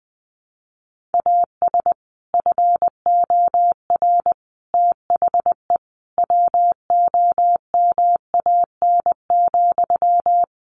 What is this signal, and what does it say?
Morse code, heard as a keyed tone, about 20 wpm, 700 Hz. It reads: AS FOR THE WOMAN,
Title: Sentences from Top 1100 Words 20wpm